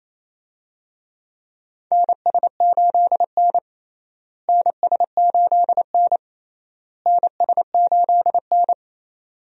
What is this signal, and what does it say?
Morse code, heard as a keyed tone, about 28 wpm, 700 Hz. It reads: NS8N NS8N NS8N
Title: USA Call Signs - 4 Characters - Repeated 3x - 28wpm